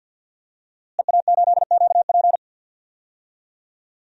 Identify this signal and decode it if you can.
Morse code, heard as a keyed tone, about 50 wpm, 700 Hz. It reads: EA8QP
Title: DXCC-100 Call Signs - 5 Characters - Repeated 3x - 50wpm